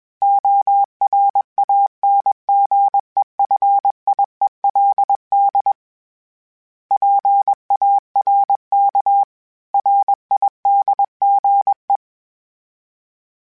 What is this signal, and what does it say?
Morse code, heard as a keyed tone, about 21 wpm, 800 Hz. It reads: ORANGEFIELD PARK RIDGE
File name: fromSMs / FromSMs mp3s / Cities@20